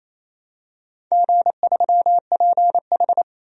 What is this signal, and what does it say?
Morse code, heard as a keyed tone, about 27 wpm, 690 Hz. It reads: G3PH
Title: DXCC-100 Call Signs - 4 Characters - 28wpm